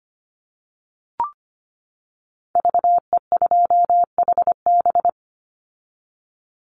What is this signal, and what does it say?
Morse code, heard as a keyed tone, about 25 wpm, 700 Hz. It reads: VE2HB
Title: Call Signs - 25wpm